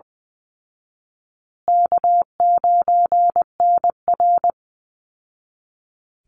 Morse code, heard as a keyed tone, about 20 wpm, 700 Hz. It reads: K9NR
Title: Easy Call Signs Repeated 3x - 20wpm